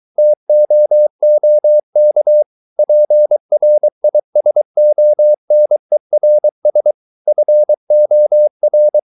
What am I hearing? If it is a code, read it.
Morse code, heard as a keyed tone, about 23 wpm, 600 Hz. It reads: TOOK PRISONERS FOR